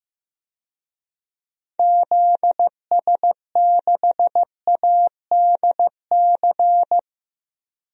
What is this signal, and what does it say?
Morse code, heard as a keyed tone, about 15 wpm, 700 Hz. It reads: ZS6ADC
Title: DXCC-100 Call Signs - 6 Characters - 15wpm